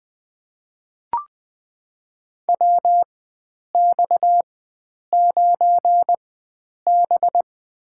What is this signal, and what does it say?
Morse code, heard as a keyed tone, about 20 wpm, spaced as for 9 wpm, 700 Hz. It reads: WX9B
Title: USA Call Signs - 4 Characters - 4x Character Spacing - 20wpm